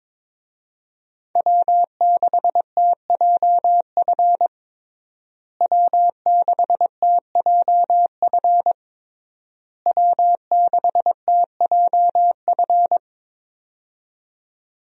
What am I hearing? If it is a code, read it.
Morse code, heard as a keyed tone, about 22 wpm, 700 Hz. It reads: W6TJF W6TJF W6TJF
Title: USA Call Signs - 5 Characters - Repeated 3x - 22wpm